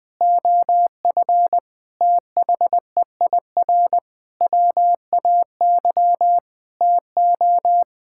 Morse code, heard as a keyed tone, about 21 wpm, 700 Hz. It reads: OF THEIR WAY TO
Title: Sentences with 6-Word Phrases - Spoken Beginning - 20wpm